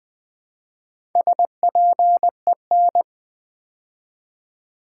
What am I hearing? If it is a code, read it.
Morse code, heard as a keyed tone, about 20 wpm, 700 Hz. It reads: SPEN
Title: Build A Word - 2 Letters At A Time - 8-Letter Words - 20wpm